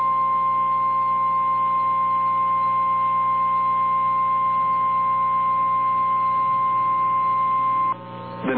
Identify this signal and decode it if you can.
Emergency Alert System (SAME) tones (the 1050 Hz attention tone). It is heard from 0.3 to 7.8 s.
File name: eas / KNCYKBIE / KNCYKBIE